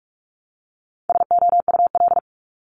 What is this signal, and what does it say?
Morse code, heard as a keyed tone, about 46 wpm, 700 Hz. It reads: SOUL
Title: Multi-letter Phonemes 45wpm